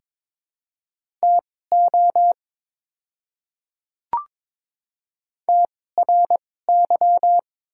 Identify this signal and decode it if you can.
Morse code, heard as a keyed tone, about 22 wpm, spaced as for 16 wpm, 700 Hz. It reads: TO TRY
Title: 2x Character Spacing - Top 200 Words - 22wpm